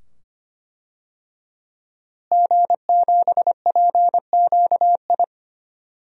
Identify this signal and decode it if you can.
Morse code, heard as a keyed tone, about 25 wpm, 700 Hz. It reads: G7PQI
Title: DXCC-100 Call Signs - 5 Characters - Repeated 2x - 25wpm